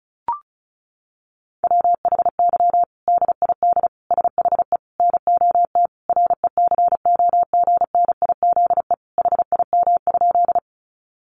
Morse code, heard as a keyed tone, about 34 wpm, 700 Hz. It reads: WHY DID SHE NOT RECOGNIZE HIM?
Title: Sentences from Top 1400 Words 35wpm